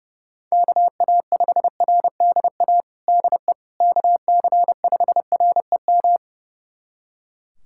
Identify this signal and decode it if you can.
Morse code, heard as a keyed tone, about 32 wpm, 700 Hz. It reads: KA5RDA DE KC5REM
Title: USA Callsign DE Callsign 30wpm